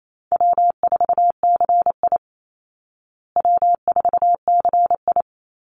Morse code, heard as a keyed tone, about 28 wpm, 700 Hz. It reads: W4CI W4CI
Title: Easy Call Signs Repeated 3x - 28wpm